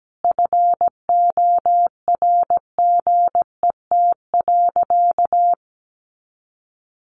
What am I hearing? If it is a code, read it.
Morse code, heard as a keyed tone, about 17 wpm, 700 Hz. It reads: FORGET.
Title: Sentences from Top 1200 Words 17wpm